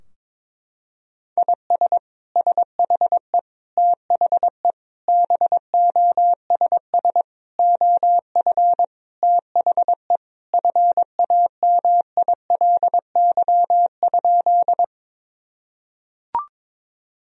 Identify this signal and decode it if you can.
Morse code, heard as a keyed tone, about 22 wpm, 700 Hz. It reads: IS SHE THE BOSS OF THE FAMILY?
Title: Sentences from Top 1000 Words - Review Part 4 - 22wpm